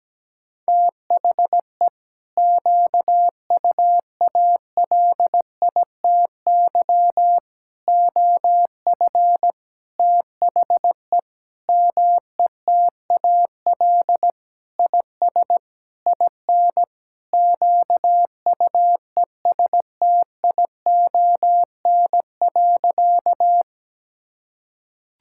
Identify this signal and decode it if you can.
Morse code, heard as a keyed tone, about 17 wpm, 700 Hz. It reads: THE QUALITY OF THE METAL IS IN QUESTION.
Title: Sentences from Top 1000 Words - Review Part 6 - 17wpm